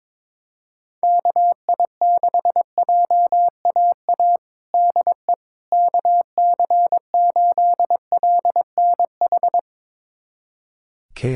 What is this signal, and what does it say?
Morse code, heard as a keyed tone, about 22 wpm, 700 Hz. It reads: KI6JAA DE KC8LNH
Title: USA Callsign DE Callsign 22wpm